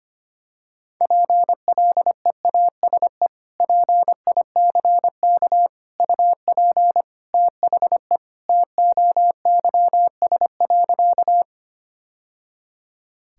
Morse code, heard as a keyed tone, about 25 wpm, 700 Hz. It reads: PLEASE PICK UP THE TOYS.
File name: practice / 25wpm / Sentences with 2-Word Phrases - 25wpm